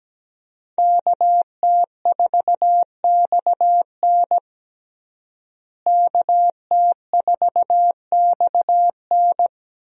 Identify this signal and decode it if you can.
Morse code, heard as a keyed tone, about 17 wpm, 700 Hz. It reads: KT4XN KT4XN
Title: USA Call Signs - 5 Characters - Repeated 3x - 17wpm